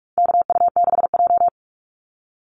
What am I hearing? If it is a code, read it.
Morse code, heard as a keyed tone, about 44 wpm, 700 Hz. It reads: KU6J
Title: NA SOTA Chaser Callsigns 45wpm